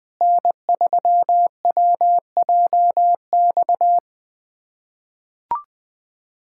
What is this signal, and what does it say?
Morse code, heard as a keyed tone, about 20 wpm, 700 Hz. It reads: N3WJX
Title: USA Callsign DE Callsign 20wpm